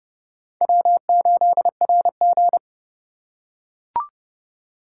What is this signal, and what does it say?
Morse code, heard as a keyed tone, about 30 wpm, 700 Hz. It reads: W8RG
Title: USA Call Signs - 4 Characters - 3x Character Spacing - 30wpm